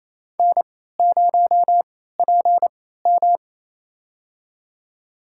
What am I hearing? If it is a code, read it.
Morse code, heard as a keyed tone, about 28 wpm, spaced as for 16 wpm, 700 Hz. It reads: N0PM
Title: USA Call Signs - 4 Characters - 3x Character Spacing - 28wpm